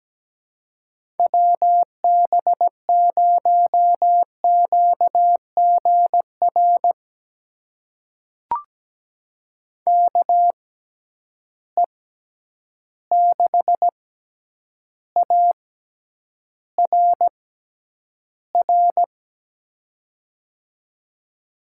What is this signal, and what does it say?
Morse code, heard as a keyed tone, about 17 wpm, spaced as for 9 wpm, 700 Hz. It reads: WB0QGR KE6ARR
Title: USA Call Signs - 6 Characters - 6x Character Spacing - 17wpm